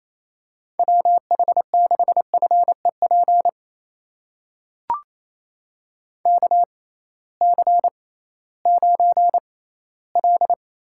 Morse code, heard as a keyed tone, about 28 wpm, spaced as for 16 wpm, 700 Hz. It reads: WH6FEP KC9L
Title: USA Call Signs - 6x Character Spacing - 28wpm